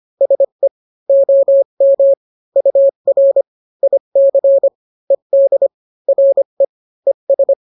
Morse code, heard as a keyed tone, about 25 wpm, 550 Hz. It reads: SE OM UR IC ED RE ES